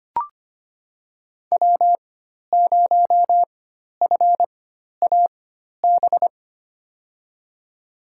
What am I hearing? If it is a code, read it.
Morse code, heard as a keyed tone, about 25 wpm, spaced as for 12 wpm, 700 Hz. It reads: W0FAB
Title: USA Call Signs - 5 Characters - 4x Character Spacing - 25wpm